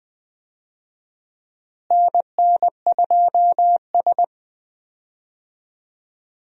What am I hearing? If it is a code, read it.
Morse code, heard as a keyed tone, about 20 wpm, 700 Hz. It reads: NN2S
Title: USA Call Signs - 4 Characters - Repeated 3x - 20wpm